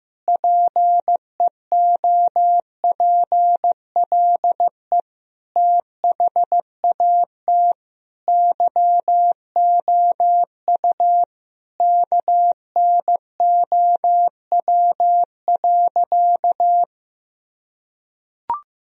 Morse code, heard as a keyed tone, about 15 wpm, 700 Hz. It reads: PEOPLE THAT YOU KNOW.
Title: Sentences from Top 1000 Words - Review Part 2 - 15wpm